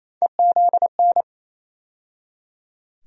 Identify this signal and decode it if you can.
Morse code, heard as a keyed tone, about 28 wpm, 700 Hz. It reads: EZN
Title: USA Call Sign Postfixes - 3 Letters 28wpm